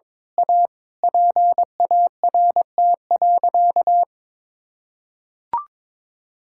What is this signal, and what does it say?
Morse code, heard as a keyed tone, about 22 wpm, 700 Hz. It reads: A PART.
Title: Sentences from Top 1000 Words - Review Part 5 - 22wpm